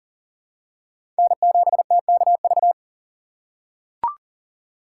Morse code, heard as a keyed tone, about 40 wpm, 700 Hz. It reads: N7TKV
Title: USA Call Signs - 5 Characters - 4x Character Spacing - 40wpm